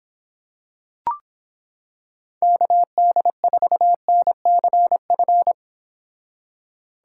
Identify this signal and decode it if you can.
Morse code, heard as a keyed tone, about 26 wpm, 700 Hz. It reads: KD4NCF